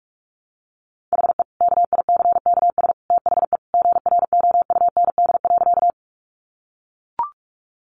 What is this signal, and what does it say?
Morse code, heard as a keyed tone, about 44 wpm, 700 Hz. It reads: HE KICKS THE GROUND.
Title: Additional Easy Phrases 45wpm